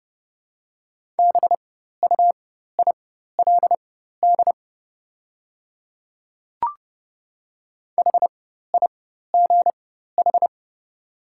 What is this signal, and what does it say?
Morse code, heard as a keyed tone, about 30 wpm, spaced as for 14 wpm, 700 Hz. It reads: BUILD HIGH
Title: 4x Character Spacing - Top 200 Words - 30wpm